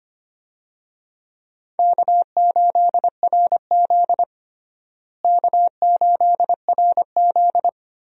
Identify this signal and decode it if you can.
Morse code, heard as a keyed tone, about 25 wpm, 700 Hz. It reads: K8RZ K8RZ
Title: USA Call Signs - 4 Characters - Repeated 3x - 25wpm